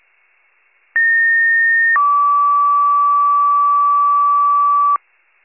Two-tone page, heard as a one-second tone, then a three-second tone, about 1850 then 1150 Hz, with noise. It is heard from 1.0 to 5.0 s.